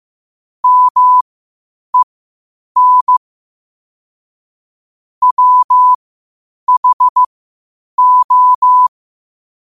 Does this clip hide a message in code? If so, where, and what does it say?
Yes: Morse code from 0.6 to 8.9 s, heard as a keyed tone, about 15 wpm, spaced as for 8 wpm, 1000 Hz. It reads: MEN WHO